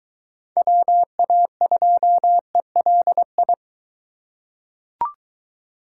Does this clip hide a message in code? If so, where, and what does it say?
Yes: Morse code from 0.6 to 3.5 s, heard as a keyed tone, about 21 wpm, 700 Hz. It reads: WA2ELI